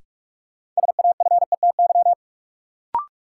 Morse code, heard as a keyed tone, about 43 wpm, 700 Hz. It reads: SAFETY
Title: Sentences from Top 700 Words - New Word Drill - 45wpm